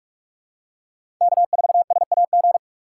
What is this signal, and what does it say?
Morse code, heard as a keyed tone, about 44 wpm, 700 Hz. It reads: K4SAG
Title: USA Call Signs - 5 Characters - 6x Character Spacing - 45wpm